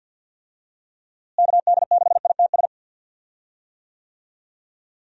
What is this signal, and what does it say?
Morse code, heard as a keyed tone, about 51 wpm, 700 Hz. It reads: KD6ITS
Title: USA Call Signs - 6 Characters - Repeated 2x - 50wpm